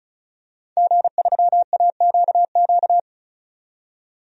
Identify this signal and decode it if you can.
Morse code, heard as a keyed tone, about 36 wpm, 700 Hz. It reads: G3AQQ
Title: DXCC-100 Call Signs - 5 Characters - 35wpm